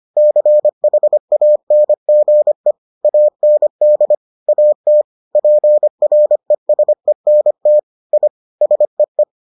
Morse code, heard as a keyed tone, about 25 wpm, 600 Hz. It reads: CHANGE AND AT PRESENT I SEE